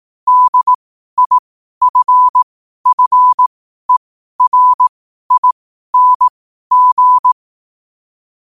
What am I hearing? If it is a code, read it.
Morse code, heard as a keyed tone, about 18 wpm, spaced as for 12 wpm, 1000 Hz. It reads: DIFFERING
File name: qotdfiles / 12wpm